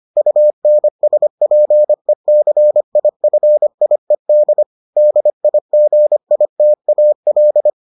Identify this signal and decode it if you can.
Morse code, heard as a keyed tone, about 25 wpm, 600 Hz. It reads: UNSPECIFIED DIGITAL